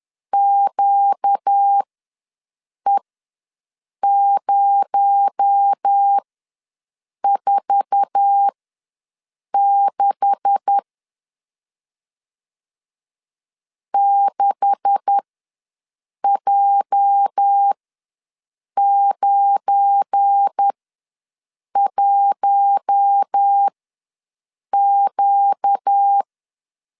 Morse code, heard as a keyed tone, about 10 wpm, spaced as for 6 wpm, 800 Hz. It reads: QE046 6J91Q
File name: LEZIONI / 9 286 ws 10wpm 4032